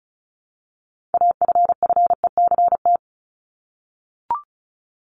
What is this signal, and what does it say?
Morse code, heard as a keyed tone, about 34 wpm, 700 Hz. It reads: AFFECT
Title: Top 751-1000 Words 35wpm